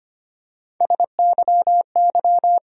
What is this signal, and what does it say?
Morse code, heard as a keyed tone, about 25 wpm, 700 Hz. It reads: SYY
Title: USA Call Sign Postfixes - 3 Letters 25wpm